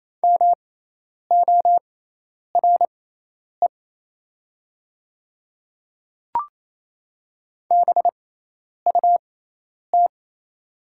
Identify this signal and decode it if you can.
Morse code, heard as a keyed tone, about 28 wpm, spaced as for 10 wpm, 700 Hz. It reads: MORE BUT
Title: 6x Character Spacing - Top 200 Words - 28wpm